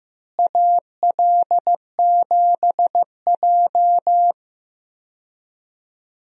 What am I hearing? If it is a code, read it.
Morse code, heard as a keyed tone, about 15 wpm, 700 Hz. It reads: AL7J
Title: USA Call Signs - 4 Characters - Repeated 2x - 15wpm